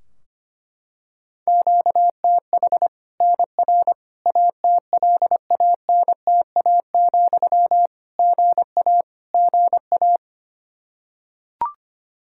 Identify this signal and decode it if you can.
Morse code, heard as a keyed tone, about 25 wpm, 700 Hz. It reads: QTH NR ATLANTA, GA GA
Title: Practice QSO - Exchange 1 - Beginning - 25wpm